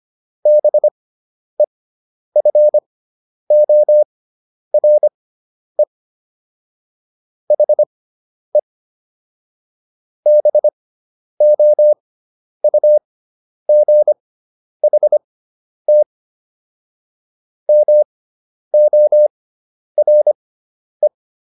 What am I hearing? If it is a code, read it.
Morse code, heard as a keyed tone, about 25 wpm, spaced as for 10 wpm, 600 Hz. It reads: BEFORE HE BOUGHT MORE